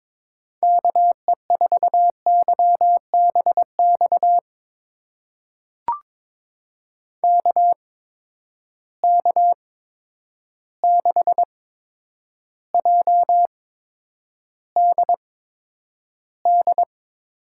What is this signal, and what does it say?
Morse code, heard as a keyed tone, about 22 wpm, spaced as for 9 wpm, 700 Hz. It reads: KE4YBX KK6JDD
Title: USA Call Signs - 6 Characters - 8x Character Spacing - 22wpm